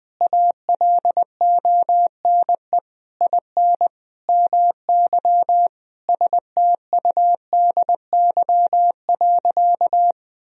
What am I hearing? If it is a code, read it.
Morse code, heard as a keyed tone, about 20 wpm, 700 Hz. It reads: ALONE IN MY STUDY.
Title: Sentences from Top 1000 Words - Review Part 1 - 20wpm